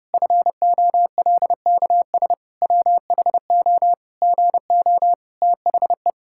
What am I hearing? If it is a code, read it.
Morse code, heard as a keyed tone, about 30 wpm, 700 Hz. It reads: FOLKS WHO GO THE
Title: Pangram Challenge v2 - 30wpm